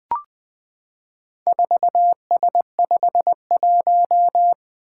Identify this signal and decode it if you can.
Morse code, heard as a keyed tone, about 20 wpm, 700 Hz. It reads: 4S51
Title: Four Letter-Number Combinations - 20wpm